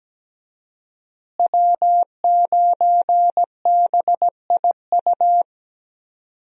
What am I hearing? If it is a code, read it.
Morse code, heard as a keyed tone, about 17 wpm, 700 Hz. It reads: W9BIU
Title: USA Call Signs - 5 Characters - Repeated 3x - 17wpm